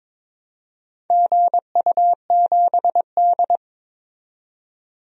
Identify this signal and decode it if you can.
Morse code, heard as a keyed tone, about 23 wpm, 700 Hz. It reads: GU7D
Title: DXCC-100 Call Signs - 4 Characters - 22wpm